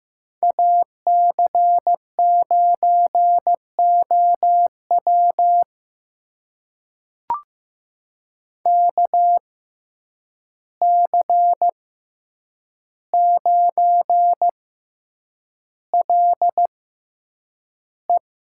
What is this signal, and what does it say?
Morse code, heard as a keyed tone, about 15 wpm, spaced as for 8 wpm, 700 Hz. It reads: AC9OW KC9LE
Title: USA Call Signs - 5 Characters - 6x Character Spacing - 15wpm